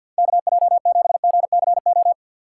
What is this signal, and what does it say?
Morse code, heard as a keyed tone, about 51 wpm, 700 Hz. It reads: KJ7GCO
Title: USA Call Signs - 4x Character Spacing - 50wpm